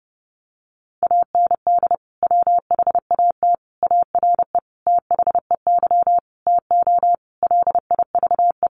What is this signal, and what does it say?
Morse code, heard as a keyed tone, about 30 wpm, 700 Hz. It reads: AND WHAT ARE THEY TO LIVE
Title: Sentences from Top 1100 Words 30wpm